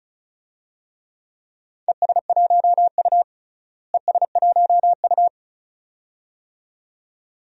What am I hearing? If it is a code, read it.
Morse code, heard as a keyed tone, about 36 wpm, 700 Hz. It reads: ES1U ES1U
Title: DXCC-100 Call Signs - 4 Characters - Repeated 2x - 35wpm